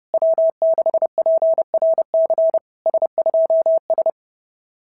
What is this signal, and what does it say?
Morse code, heard as a keyed tone, about 30 wpm, 690 Hz. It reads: W6PRC S2S
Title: SOTA QSO Practice 30wpm